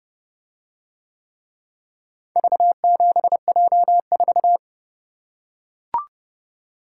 Morse code, heard as a keyed tone, about 29 wpm, 700 Hz. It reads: V7J4